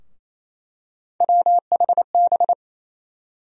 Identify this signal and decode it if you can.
Morse code, heard as a keyed tone, about 28 wpm, 700 Hz. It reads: WHB
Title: USA Call Sign Postfixes - 3 Letters 28wpm